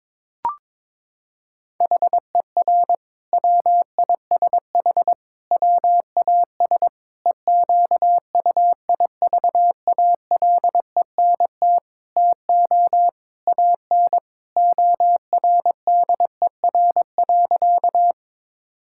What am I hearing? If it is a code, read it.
Morse code, heard as a keyed tone, about 22 wpm, 700 Hz. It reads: HER WISH WAS EQUIVALENT TO AN ORDER.
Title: Sentences from Top 1600 Words 22wpm